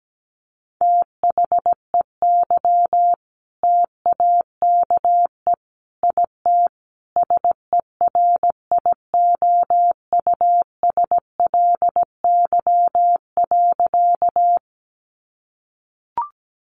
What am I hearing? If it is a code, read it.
Morse code, heard as a keyed tone, about 17 wpm, 700 Hz. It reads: THEY TAKE IT SERIOUSLY.
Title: Sentences from Top 1200 Words 17wpm